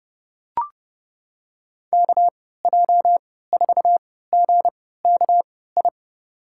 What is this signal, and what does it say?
Morse code, heard as a keyed tone, about 30 wpm, spaced as for 17 wpm, 700 Hz. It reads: KJ4GKI
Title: USA Call Signs - 6 Characters - 3x Character Spacing - 30wpm